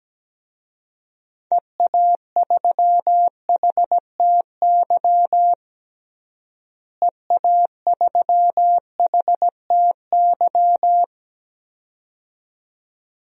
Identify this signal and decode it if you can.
Morse code, heard as a keyed tone, about 17 wpm, 700 Hz. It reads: EA3HTY EA3HTY
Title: DXCC-100 Call Signs - 6 Characters - Repeated 2x - 17wpm